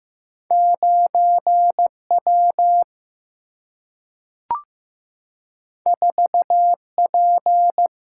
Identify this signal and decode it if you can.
Morse code, heard as a keyed tone, about 15 wpm, 700 Hz. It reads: 9W 4P
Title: Single-Letter Single-Number 15wpm